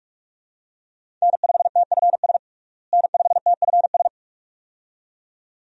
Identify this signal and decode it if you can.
Morse code, heard as a keyed tone, about 46 wpm, 700 Hz. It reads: N5TFS N5TFS
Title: USA Call Signs - 5 Characters - Repeated 2x - 45wpm